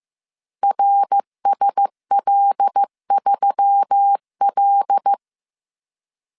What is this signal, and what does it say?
Morse code, heard as a keyed tone, about 15 wpm, 800 Hz. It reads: RSL3L